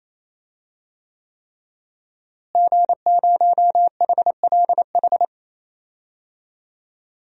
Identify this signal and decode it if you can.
Morse code, heard as a keyed tone, about 28 wpm, 700 Hz. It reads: G0HLH
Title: DXCC-100 Call Signs - 5 Characters - Repeated 3x - 28wpm